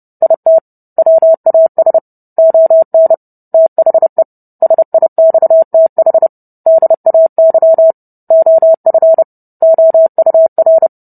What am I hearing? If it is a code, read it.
Morse code, heard as a keyed tone, about 30 wpm, 650 Hz. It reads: IT WAS ON THE SIXTH DAY OF OUR